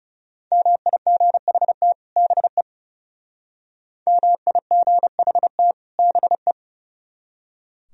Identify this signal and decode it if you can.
Morse code, heard as a keyed tone, about 38 wpm, 700 Hz. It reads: MIGHT BE MIGHT BE